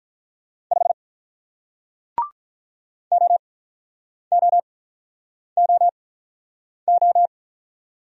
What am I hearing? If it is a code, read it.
Morse code, heard as a keyed tone, about 48 wpm, spaced as for 9 wpm, 700 Hz. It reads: 5 OOOO